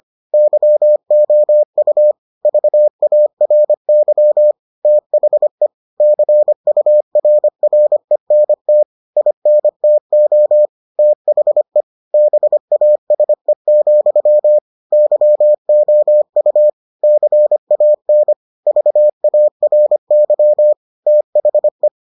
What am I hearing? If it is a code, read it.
Morse code, heard as a keyed tone, about 25 wpm, 600 Hz. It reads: YOU VARY THE CURRENT INTO THE BASE, YOU CAN VARY THE